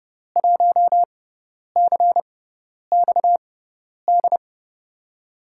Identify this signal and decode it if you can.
Morse code, heard as a keyed tone, about 30 wpm, spaced as for 10 wpm, 700 Hz. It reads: 1CXD
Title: USA Call Signs - 6 Characters - 6x Character Spacing - 30wpm